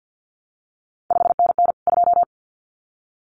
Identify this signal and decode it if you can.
Morse code, heard as a keyed tone, about 50 wpm, 700 Hz. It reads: 5NN 2
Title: CQ WW DX Contest Exchange - 50wpm